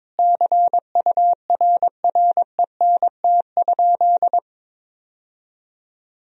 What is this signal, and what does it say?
Morse code, heard as a keyed tone, about 22 wpm, 700 Hz. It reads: CURRENT?
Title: Sentences from Top 1000 Words - Review Part 4 - 22wpm